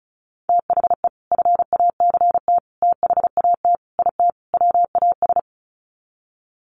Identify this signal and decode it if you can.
Morse code, heard as a keyed tone, about 34 wpm, 700 Hz. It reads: THE FACT THAT IT WAS
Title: Sets of 5 Words Encore2 35wpm